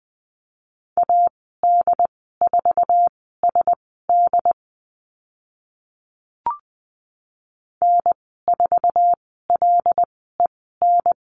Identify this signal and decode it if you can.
Morse code, heard as a keyed tone, about 20 wpm, spaced as for 14 wpm, 700 Hz. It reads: AD4SD N4LEN
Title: 076 Lesson 13g - Letter L - Call Signs 20wpm